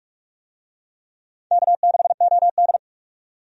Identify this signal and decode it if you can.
Morse code, heard as a keyed tone, about 46 wpm, 700 Hz. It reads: K6OD
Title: USA Call Signs - 4 Characters - 45wpm